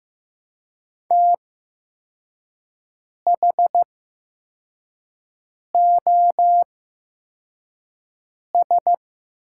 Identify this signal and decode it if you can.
Morse code, heard as a keyed tone, about 15 wpm, spaced as for 4 wpm, 700 Hz. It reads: THOS
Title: 8x Character Spacing - Top 500 Words - 15wpm